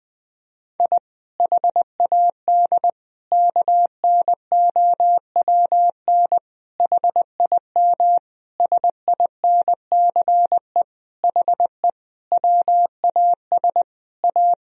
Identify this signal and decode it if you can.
Morse code, heard as a keyed tone, about 20 wpm, 700 Hz. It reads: I HAD KNOWN HIM SINCE HE WAS A
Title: Sentences from Top 1000 Words - Review Part 3 - 20wpm